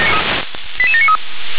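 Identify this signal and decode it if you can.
Five-tone selective call (ZVEI), heard as quick runs of five tones, tones of about 70 ms. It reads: 88793
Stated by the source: ZVEI five-tone selective call 88776, received with an SDR.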